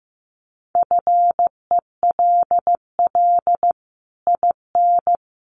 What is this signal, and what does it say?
Morse code, heard as a keyed tone, about 15 wpm, 700 Hz. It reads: FELL IN
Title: Sets of 5 Words Encore2 15wpm